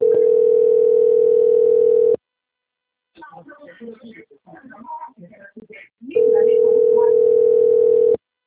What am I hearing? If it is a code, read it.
Ringback tone (North America), 440 + 480 Hz, with speech.
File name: out- - - - - .69014